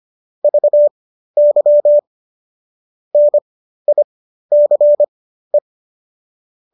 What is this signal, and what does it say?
Morse code, heard as a keyed tone, about 25 wpm, spaced as for 13 wpm, 590 Hz. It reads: VY NICE